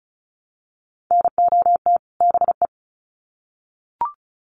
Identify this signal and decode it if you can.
Morse code, heard as a keyed tone, about 34 wpm, 700 Hz. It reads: NOT BE
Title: Sets of 2 Words 35wpm